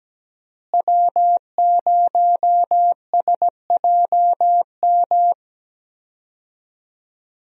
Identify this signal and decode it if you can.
Morse code, heard as a keyed tone, about 17 wpm, 700 Hz. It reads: W0SJM
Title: USA Call Signs - 5 Characters - Repeated 3x - 17wpm